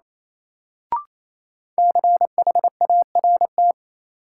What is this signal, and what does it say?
Morse code, heard as a keyed tone, about 28 wpm, 700 Hz. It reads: CHART
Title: Sentences from Top 1300 Words - New Word Drill Alt - 28wpm